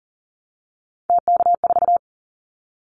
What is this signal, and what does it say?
Morse code, heard as a keyed tone, about 40 wpm, 700 Hz. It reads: TK4
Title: Callsign Prefixes for Top 100 DXCC - 40wpm